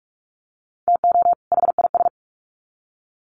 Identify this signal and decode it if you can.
Morse code, heard as a keyed tone, about 44 wpm, 700 Hz. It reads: TO HIS
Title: Sets of 2 Words 45wpm